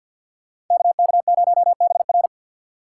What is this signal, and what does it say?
Morse code, heard as a keyed tone, about 48 wpm, 690 Hz. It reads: KK0BR
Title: Speed Racing - USA 2x2 Call Signs 50wpm